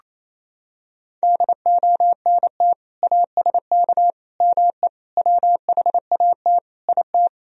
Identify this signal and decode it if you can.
Morse code, heard as a keyed tone, about 28 wpm, 700 Hz. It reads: DONT ASK ME WHAT IT
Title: Sentences from Top 1000 Words - Review Part 3 - 28wpm